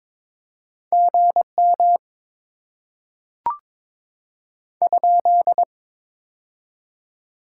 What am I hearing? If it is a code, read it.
Morse code, heard as a keyed tone, about 22 wpm, 700 Hz. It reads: GM ?
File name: Top 90 QSO Elements - 6x Character Spacing - 22wpm